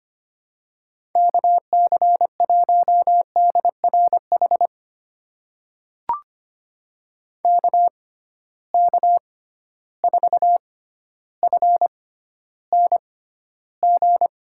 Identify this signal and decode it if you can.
Morse code, heard as a keyed tone, about 25 wpm, spaced as for 13 wpm, 700 Hz. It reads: KC1DRH KK4FNG
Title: USA Call Signs - 6x Character Spacing - 25wpm